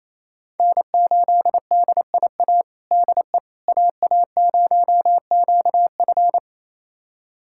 Morse code, heard as a keyed tone, about 28 wpm, 700 Hz. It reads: N8DIA DE AA0QF
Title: USA Callsign DE Callsign - 5 Characters - 28wpm